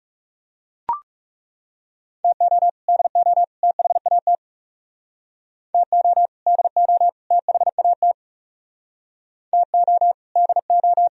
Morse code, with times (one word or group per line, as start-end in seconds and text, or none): 2.2-2.7 TO
2.9-3.4 DO
3.6-4.4 THAT
5.7-6.3 TO
6.5-7.1 DO
7.3-8.1 THAT
9.5-10.1 TO
10.4-11.1 DO